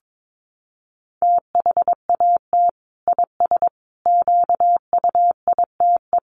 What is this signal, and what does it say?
Morse code, heard as a keyed tone, about 22 wpm, 700 Hz. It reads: THAT IS QUITE
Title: Sentences from Top 1100 Words 22wpm